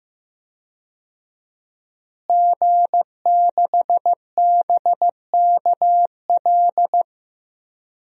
Morse code, heard as a keyed tone, about 15 wpm, 700 Hz. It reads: G6BKL
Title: DXCC-100 Call Signs - 5 Characters - Repeated 3x - 15wpm